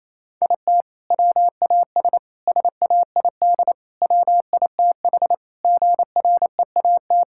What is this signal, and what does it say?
Morse code, heard as a keyed tone, about 28 wpm, 700 Hz. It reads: IT WAS SAID WITH GREAT
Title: Sentences from Top 1600 Words 28wpm